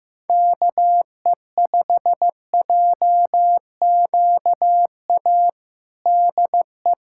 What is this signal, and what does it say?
Morse code, heard as a keyed tone, about 15 wpm, 700 Hz. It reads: KE5JQA DE
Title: USA Callsign DE Callsign 15wpm